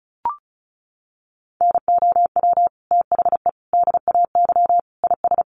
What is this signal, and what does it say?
Morse code, heard as a keyed tone, about 34 wpm, 700 Hz. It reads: NOW THE DAY IS
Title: Sentences from Top 1000 Words 35wpm